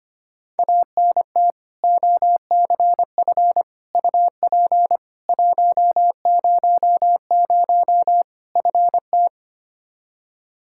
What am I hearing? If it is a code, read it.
Morse code, heard as a keyed tone, about 25 wpm, 700 Hz. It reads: ANT OCF UP 100 FT
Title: Practice QSO - Exchange 2 - Info - 25wpm